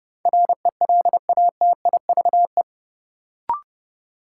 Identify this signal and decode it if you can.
Morse code, heard as a keyed tone, about 30 wpm, 700 Hz. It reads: RELATIVE
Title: Sentences from Top 1400 Words - New Word Drill Alt - 30wpm